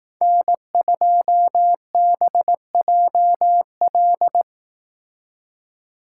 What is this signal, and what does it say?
Morse code, heard as a keyed tone, about 18 wpm, 700 Hz. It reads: N2BJL
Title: Speed Racing - USA Call Signs 15wpm